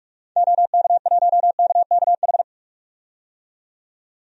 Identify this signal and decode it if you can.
Morse code, heard as a keyed tone, about 42 wpm, 700 Hz. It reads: OK1KKH
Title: DXCC-100 Call Signs - 6 Characters - 45wpm